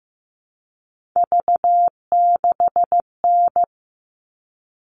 Morse code, heard as a keyed tone, about 15 wpm, 700 Hz. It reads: V6N
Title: All Cut Numbers - 3-Digit 15wpm